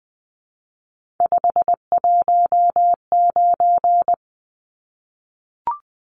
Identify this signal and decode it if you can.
Morse code, heard as a keyed tone, about 20 wpm, 700 Hz. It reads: 519
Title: Numbers - 3-Digit 20wpm